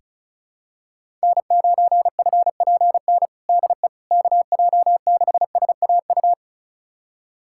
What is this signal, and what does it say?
Morse code, heard as a keyed tone, about 36 wpm, 700 Hz. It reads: N9FPN DE KJ6SAU
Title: USA Callsign DE Callsign 35wpm